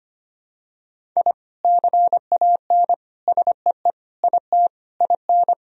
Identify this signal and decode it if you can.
Morse code, heard as a keyed tone, about 25 wpm, 700 Hz. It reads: I CAN SEE IT IN
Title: Sentences from Top 1000 Words - Review Part 1 - 25wpm